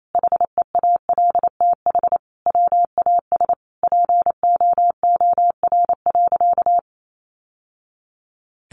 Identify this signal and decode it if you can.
Morse code, heard as a keyed tone, about 28 wpm, 700 Hz. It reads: HEALTH WAS POOR.
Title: Sentences from Top 1000 Words 28wpm